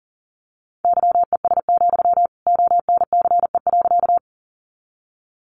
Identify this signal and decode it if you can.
Morse code, heard as a keyed tone, about 40 wpm, 700 Hz. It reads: YES, ONCE.
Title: Sentences from Top 500 Words 40wpm